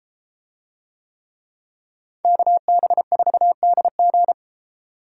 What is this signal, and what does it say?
Morse code, heard as a keyed tone, about 31 wpm, 700 Hz. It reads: KB4DG